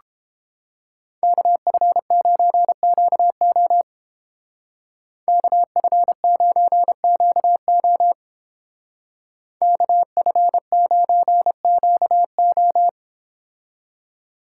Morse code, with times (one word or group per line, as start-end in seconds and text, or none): 1.2-3.8 KF9QO
5.3-8.1 KF9QO
9.6-12.9 KF9QO